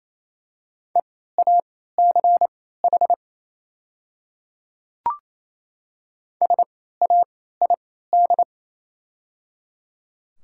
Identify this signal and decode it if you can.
Morse code, heard as a keyed tone, about 28 wpm, spaced as for 16 wpm, 700 Hz. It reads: EACH SAID
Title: 3x Character Spacing - Top 200 Words - 28wpm